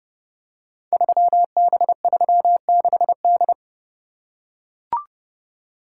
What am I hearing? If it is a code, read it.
Morse code, heard as a keyed tone, about 30 wpm, 700 Hz. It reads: 3B36D
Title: Code Groups - 5 Letter:Numbers - 30wpm